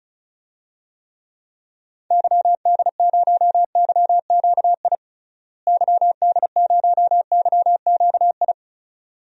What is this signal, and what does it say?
Morse code, heard as a keyed tone, about 36 wpm, 700 Hz. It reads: YD0YQI YD0YQI
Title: DXCC-100 Call Signs - 6 Characters - Repeated 2x - 35wpm